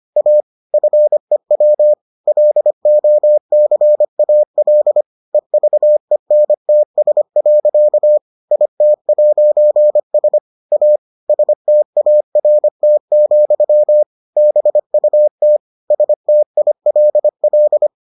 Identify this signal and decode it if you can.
Morse code, heard as a keyed tone, about 25 wpm, 600 Hz. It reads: A FEW LOCAL EVENTS. IT'S A START, BUT STILL